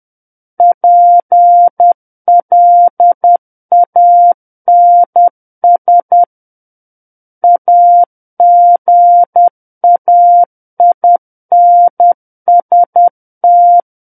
Morse code, heard as a keyed tone, about 10 wpm, 700 Hz. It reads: PLANS AGAINST
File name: Book One / 01-Chapter 1 - 10 WPM